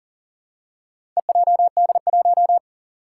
Morse code, heard as a keyed tone, about 40 wpm, 700 Hz. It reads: EJD1
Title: Four Challenging Letter-Number Combinations 40wpm